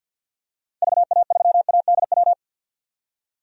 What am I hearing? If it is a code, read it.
Morse code, heard as a keyed tone, about 49 wpm, 700 Hz. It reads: VA3ADW
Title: DXCC-100 Call Signs - 6 Characters - 50wpm